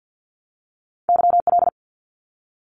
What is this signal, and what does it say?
Morse code, heard as a keyed tone, about 50 wpm, 700 Hz. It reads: YL
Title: Abbreviations Repeated 3x - 50wpm